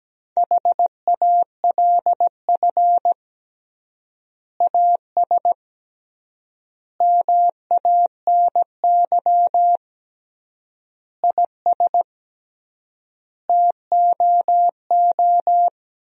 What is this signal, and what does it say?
Morse code, heard as a keyed tone, about 17 wpm, 700 Hz. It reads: HALF AS MANY IS TOO
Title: 3x Word-Spacing - Sentences from Top 400 Words - 17wpm